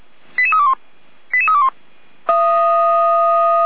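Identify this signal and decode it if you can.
Five-tone selective call (ZVEI), heard as quick runs of five tones, tones of about 70 ms. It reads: 88321 88321
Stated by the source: ZVEI five-tone call 88124 received by SDR, recorded by feuerwehr-orlamuende.org fire brigade.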